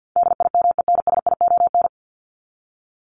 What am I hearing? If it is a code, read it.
Morse code, heard as a keyed tone, about 50 wpm, 700 Hz. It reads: DIMENSION
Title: Top 1501-1750 Words 50wpm